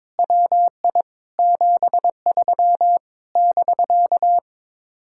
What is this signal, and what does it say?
Morse code, heard as a keyed tone, about 22 wpm, 700 Hz. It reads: WI 73 <BK>
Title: Speed-Racing Common QSO Phrases - 22wpm